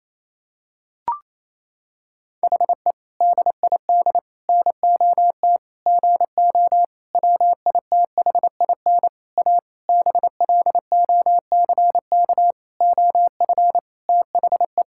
Morse code, heard as a keyed tone, about 28 wpm, 700 Hz. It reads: HE DID NOT GO WITHIN A BLOCK OF THE
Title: Sentences from Top 1600 Words 28wpm